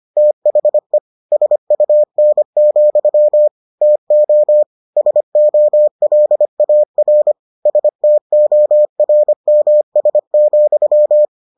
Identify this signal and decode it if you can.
Morse code, heard as a keyed tone, about 25 wpm, 600 Hz. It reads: THE SUN, TO SOLAR STORMS,